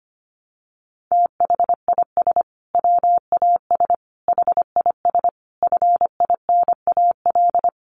Morse code, heard as a keyed tone, about 25 wpm, 700 Hz. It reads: THIS WAS HIS FINAL
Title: Sentences from Top 1300 Words 25wpm